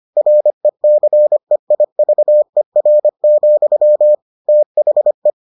Morse code, heard as a keyed tone, about 25 wpm, 600 Hz. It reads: RECEIVER, THE